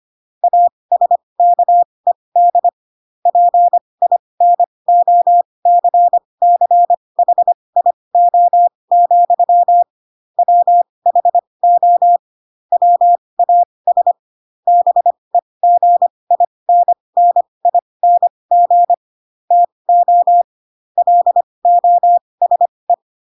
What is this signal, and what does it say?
Morse code, heard as a keyed tone, about 25 wpm, spaced as for 20 wpm, 700 Hz. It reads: ASKED PINOCCHIO, WHO WAS BEGINNING TO LOSE
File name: Pinocchio / Pinocchio ch4 15 25 ik1hge